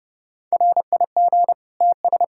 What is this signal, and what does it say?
Morse code, heard as a keyed tone, about 30 wpm, 700 Hz. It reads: RIG TS
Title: Practice QSO - Exchange 2 - Info - 30wpm